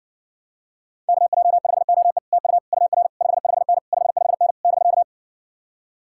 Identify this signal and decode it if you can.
Morse code, heard as a keyed tone, about 67 wpm, 700 Hz. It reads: KJ5OE TU UR 55N 55N <BK>